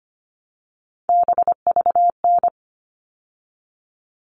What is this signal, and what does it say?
Morse code, heard as a keyed tone, about 25 wpm, 700 Hz. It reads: BVN
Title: All Cut Numbers - 3-Digit 25wpm